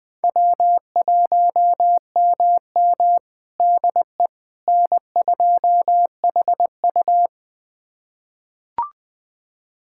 Morse code, heard as a keyed tone, about 21 wpm, 700 Hz. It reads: W1MM DE N2HU
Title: USA Callsign DE Callsign - 4 Characters - 20wpm